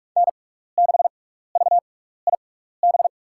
Morse code, heard as a keyed tone, about 44 wpm, spaced as for 15 wpm, 700 Hz. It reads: N6VIB
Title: USA Call Signs - 6x Character Spacing - 45wpm